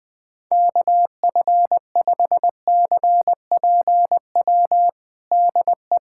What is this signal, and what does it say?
Morse code, heard as a keyed tone, about 20 wpm, 700 Hz. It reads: KF5CPW DE
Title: USA Callsign DE Callsign - 6 Characters - 20wpm